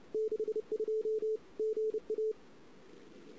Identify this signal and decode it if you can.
Morse code, heard as a keyed tone, about 30 wpm, 440 Hz. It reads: 62 GA
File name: CW / 2008-ss-cw-k5tr-op-n5rz -16:00